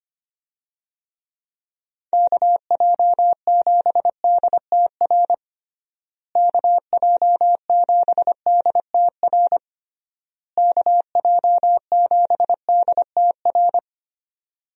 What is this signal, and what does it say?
Morse code, heard as a keyed tone, about 25 wpm, 700 Hz. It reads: KJ7DTR KJ7DTR KJ7DTR
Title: USA Call Signs - 6 Characters - Repeated 3x - 25wpm